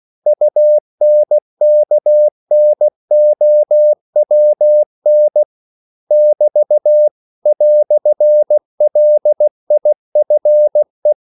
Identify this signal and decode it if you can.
Morse code, heard as a keyed tone, about 16 wpm, 600 Hz. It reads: UNKNOWN <BT> "LIFE
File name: CW / Practice / EnglishQuotations0116